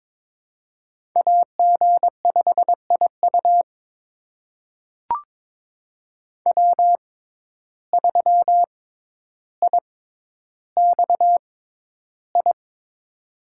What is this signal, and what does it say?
Morse code, heard as a keyed tone, about 22 wpm, spaced as for 11 wpm, 700 Hz. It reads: AG5IU W3IXI
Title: USA Call Signs - 5 Characters - 6x Character Spacing - 22wpm